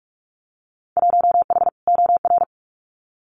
Morse code, heard as a keyed tone, about 44 wpm, 700 Hz. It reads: 1H OR
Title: Winter Field Day Exchanges 45wpm